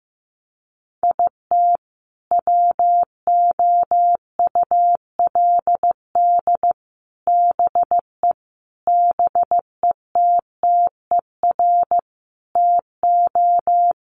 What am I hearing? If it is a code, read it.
Morse code, heard as a keyed tone, about 15 wpm, 700 Hz. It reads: IT WOULD BE BETTER TO
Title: Sentences from Top 1500 Words 15wpm